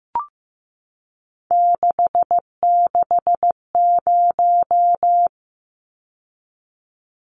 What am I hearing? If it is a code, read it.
Morse code, heard as a keyed tone, about 15 wpm, 700 Hz. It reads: 660
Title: Numbers - 3-Digit 15wpm